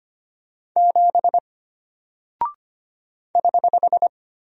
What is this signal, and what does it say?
Morse code, heard as a keyed tone, about 25 wpm, 700 Hz. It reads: 7 <HH>
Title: Single Character (Letter:Number:Punctuation:Prosign) Alt 25wpm